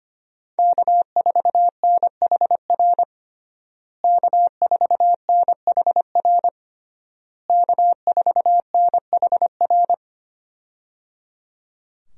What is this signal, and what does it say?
Morse code, heard as a keyed tone, about 25 wpm, 700 Hz. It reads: K4NHR K4NHR K4NHR
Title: USA Call Signs - 5 Characters - Repeated 3x - 25wpm